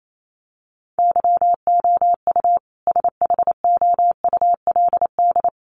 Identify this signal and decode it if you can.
Morse code, heard as a keyed tone, about 28 wpm, 700 Hz. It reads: YOU SHOULD
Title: Sentences from Top 300 Words 28wpm